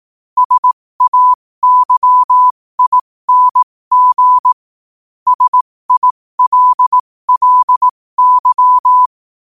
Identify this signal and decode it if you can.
Morse code, heard as a keyed tone, about 18 wpm, spaced as for 15 wpm, 1000 Hz. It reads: SAYING SILLY
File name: qotdfiles / 15wpm